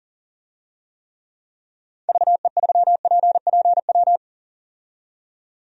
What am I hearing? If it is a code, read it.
Morse code, heard as a keyed tone, about 40 wpm, 700 Hz. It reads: VE3PPW
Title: DXCC-100 Call Signs - 6 Characters - 40wpm